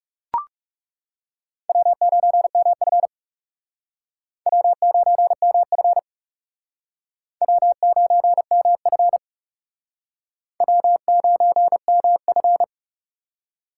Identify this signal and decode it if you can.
Morse code, heard as a keyed tone, about 39 wpm, 700 Hz. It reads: W9MF W9MF W9MF W9MF